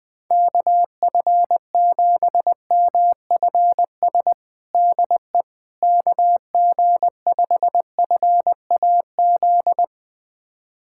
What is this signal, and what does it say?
Morse code, heard as a keyed tone, about 21 wpm, 700 Hz. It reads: KF7MFS DE KG5FAZ
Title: USA Callsign DE Callsign - 6 Characters - 20wpm